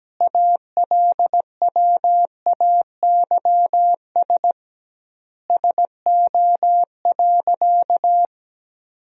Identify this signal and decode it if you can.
Morse code, heard as a keyed tone, about 17 wpm, 700 Hz. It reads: ALWAYS SO.
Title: 2x Word-Spacing - Sentences from Top 300 Words - 17wpm